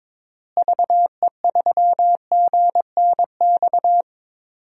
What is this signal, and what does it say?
Morse code, heard as a keyed tone, about 22 wpm, 700 Hz. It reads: VE3GNX
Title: DXCC-100 Call Signs - 6 Characters - Repeated 2x - 22wpm